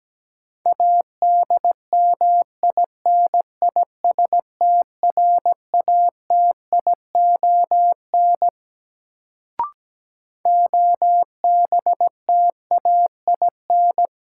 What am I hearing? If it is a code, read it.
Morse code, heard as a keyed tone, about 17 wpm, 700 Hz. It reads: ADMINISTRATION OBTAIN
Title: Sentences from Top 1300 Words - New Word Drill - 17wpm